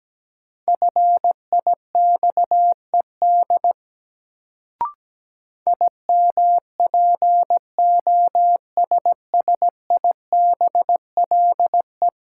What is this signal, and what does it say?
Morse code, heard as a keyed tone, about 17 wpm, 700 Hz. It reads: FIXED IMPOSSIBLE
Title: Sentences from Top 1100 Words - New Word Drill Alt - 17wpm